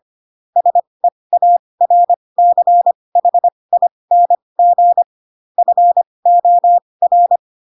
Morse code, heard as a keyed tone, about 25 wpm, spaced as for 20 wpm, 700 Hz. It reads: SEARCHING FOR